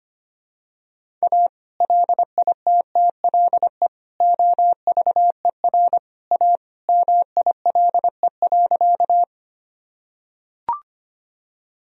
Morse code, heard as a keyed tone, about 25 wpm, 700 Hz. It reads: A LITTLE OVER A MILE.
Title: Sentences from Top 1000 Words - Review Part 1 - 25wpm